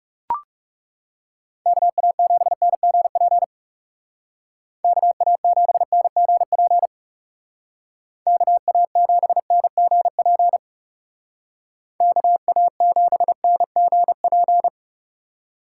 Morse code, heard as a keyed tone, about 38 wpm, 700 Hz. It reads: KA7NGP KA7NGP KA7NGP KA7NGP